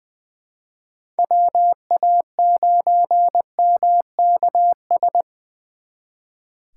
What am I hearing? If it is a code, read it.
Morse code, heard as a keyed tone, about 24 wpm, 700 Hz. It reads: WA9MKS
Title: Speed Racing - USA 2x3 Call Signs 20wpm